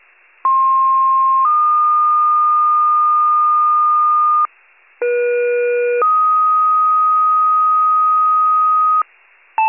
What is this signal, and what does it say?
Two-tone page, heard as a one-second tone, then a three-second tone, about 1060 then 1220 Hz, about 480 then 1250 Hz. It is heard from 0.4 to 9.0 s.
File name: pager / 23 FIRE / wcalert